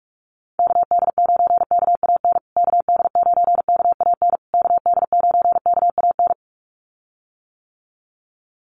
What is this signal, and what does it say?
Morse code, heard as a keyed tone, about 44 wpm, 700 Hz. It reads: KD9KAN KD9KAN KD9KAN
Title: USA Call Signs Repeated 3x 45wpm